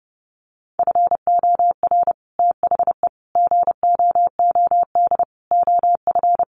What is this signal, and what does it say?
Morse code, heard as a keyed tone, about 30 wpm, 700 Hz. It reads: FOR THE GOOD OF
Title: Sets of 5 Words Encore2 30wpm